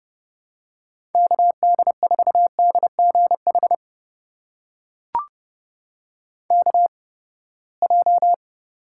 Morse code, heard as a keyed tone, about 32 wpm, spaced as for 21 wpm, 700 Hz. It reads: KD4DGH KJ
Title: USA Call Signs - 6 Characters - 8x Character Spacing - 30wpm